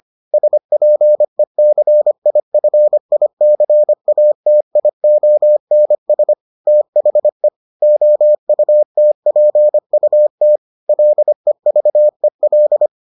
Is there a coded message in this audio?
Yes: Morse code from 0.3 to 12.9 s, heard as a keyed tone, about 25 wpm, 600 Hz. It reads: SPECIFICATIONS THE OUTPUT LEVEL